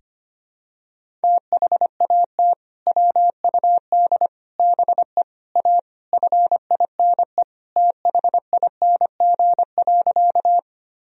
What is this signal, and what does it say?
Morse code, heard as a keyed tone, about 25 wpm, 700 Hz. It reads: THAT WUD BE A FINE THING.
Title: Sentences from Top 500 Words and Abbreviations 25wpm